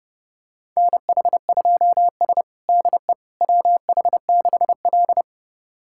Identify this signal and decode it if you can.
Morse code, heard as a keyed tone, about 29 wpm, 700 Hz. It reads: NH2S DE WH6L
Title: USA Callsign DE Callsign - 4 Characters - 30wpm